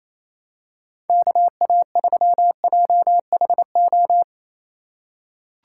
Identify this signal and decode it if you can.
Morse code, heard as a keyed tone, about 28 wpm, 700 Hz. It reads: KA3JHO
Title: USA Call Signs - 3x Character Spacing - 28wpm